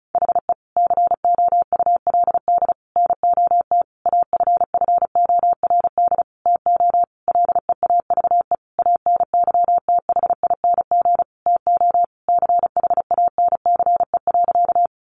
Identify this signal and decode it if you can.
Morse code, heard as a keyed tone, about 34 wpm, 700 Hz. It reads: HE COULD NOT AFFORD TO LEAVE ANYTHING TO CHANCE.
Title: Sentences from Top 1100 Words 35wpm